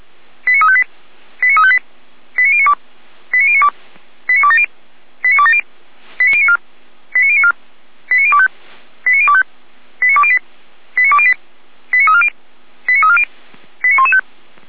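Five-tone selective call (ZVEI), heard as quick runs of five tones, tones of about 70 ms. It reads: 89358 89358 89092 89092 89270 89270 89094 89094 89025 89025 89208 89208 89340 89340 89115
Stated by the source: Five-tone ZVEI selective call received on SDR, from the Feuerwehr Orlamünde (fire brigade) audio server.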